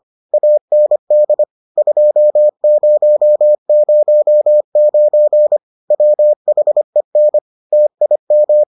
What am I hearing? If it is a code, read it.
Morse code, heard as a keyed tone, about 25 wpm, 600 Hz. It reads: AND 2009 WHEN TIM